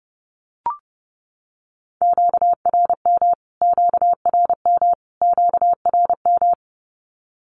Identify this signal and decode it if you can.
Morse code, heard as a keyed tone, about 29 wpm, 700 Hz. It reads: QRM QRM QRM
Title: Q-Codes Repeated 3x 30wpm